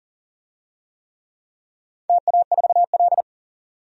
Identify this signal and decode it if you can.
Morse code, heard as a keyed tone, about 41 wpm, 700 Hz. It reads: TA4L
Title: DXCC-100 Call Signs - 4 Characters - 40wpm